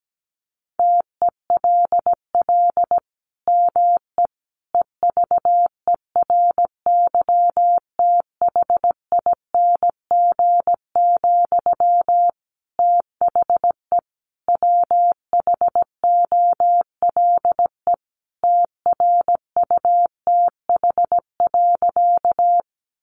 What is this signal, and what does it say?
Morse code, heard as a keyed tone, about 17 wpm, 700 Hz. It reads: TELL ME EVERYTHING, THE WHOLE TRUTH.
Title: Sentences from Top 800 Words 17wpm